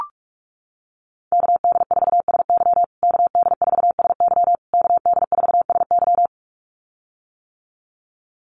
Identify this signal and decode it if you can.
Morse code, heard as a keyed tone, about 44 wpm, 700 Hz. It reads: KD4SY KD4SY KD4SY
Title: USA Call Signs Repeated 3x 45wpm